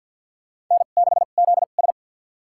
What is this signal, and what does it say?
Morse code, heard as a keyed tone, about 47 wpm, spaced as for 35 wpm, 700 Hz. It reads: N6ZS
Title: USA Call Signs - 4 Characters - 2x Character Spacing - 50wpm